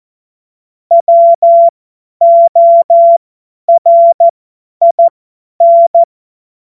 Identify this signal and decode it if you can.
Morse code, heard as a keyed tone, about 14 wpm, spaced as for 10 wpm, 680 Hz. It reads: WORIN
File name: Morsekurs / Witze / Witz 31. Mai 2022 50 F70